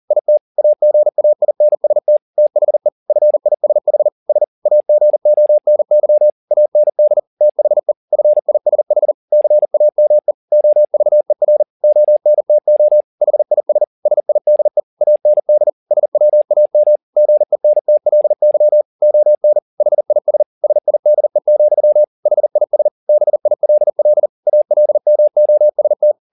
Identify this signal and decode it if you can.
Morse code, heard as a keyed tone, about 40 wpm, 600 Hz. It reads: IT AGAINST THE FISH S AGONY AND THE FISH CAME OVER ONTO HIS SIDE AND SWAM GENTLY ON HIS SIDE, HIS BILL ALMOST
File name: the old man and the sea 2 40wpm